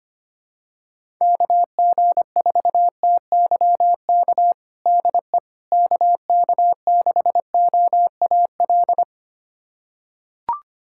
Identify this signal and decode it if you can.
Morse code, heard as a keyed tone, about 25 wpm, 700 Hz. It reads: KG4TYK DE KK6OAL
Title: USA Callsign DE Callsign - 6 Characters - 25wpm